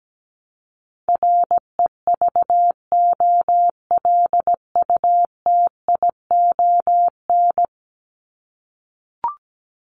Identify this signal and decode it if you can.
Morse code, heard as a keyed tone, about 17 wpm, 700 Hz. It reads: REVOLUTION
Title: Top 1751-2000 Words 17wpm